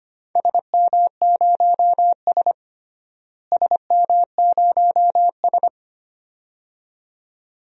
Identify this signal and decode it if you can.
Morse code, heard as a keyed tone, about 25 wpm, 700 Hz. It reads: SM0S SM0S
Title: DXCC-100 Call Signs - 4 Characters - Repeated 2x - 25wpm